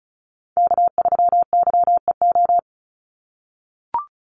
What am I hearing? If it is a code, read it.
Morse code, heard as a keyed tone, about 34 wpm, 700 Hz. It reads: K3YEO
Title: Call Signs - 35wpm